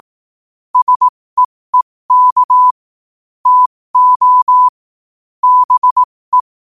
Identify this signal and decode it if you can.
Morse code, heard as a keyed tone, about 18 wpm, spaced as for 15 wpm, 1000 Hz. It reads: SEEK TO BE
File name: qotdfiles / 15wpm